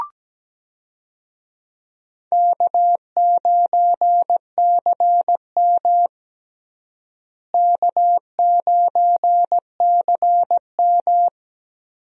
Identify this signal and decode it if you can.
Morse code, heard as a keyed tone, about 17 wpm, 700 Hz. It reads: K9CM K9CM
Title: USA Call Signs - 4 Characters - Repeated 2x - 17wpm